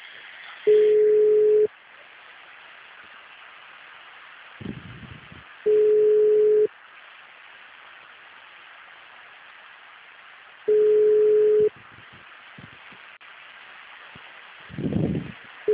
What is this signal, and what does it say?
Ringback tone (Europe, 425 Hz).